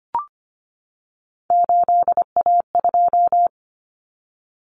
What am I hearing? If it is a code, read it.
Morse code, heard as a keyed tone, about 25 wpm, 700 Hz. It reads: 8A2
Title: Common Cut Numbers - 3-Digit 25wpm